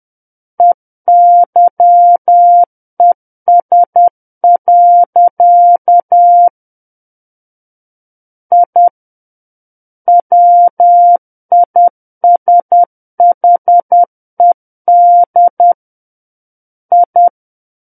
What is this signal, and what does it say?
Morse code, heard as a keyed tone, about 10 wpm, 700 Hz. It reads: EYES. I WISHED I
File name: Book One / 01-Chapter 1 - 10 WPM